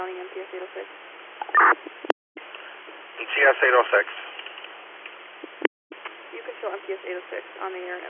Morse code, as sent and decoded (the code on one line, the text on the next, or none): none